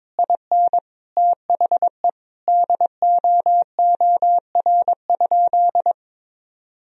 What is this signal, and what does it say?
Morse code, heard as a keyed tone, about 22 wpm, 700 Hz. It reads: IN THE DOOR?
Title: Sentences from Top 1000 Words - Review Part 4 - 22wpm